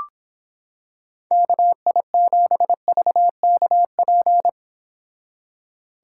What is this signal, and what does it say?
Morse code, heard as a keyed tone, about 26 wpm, 700 Hz. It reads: KI7VKP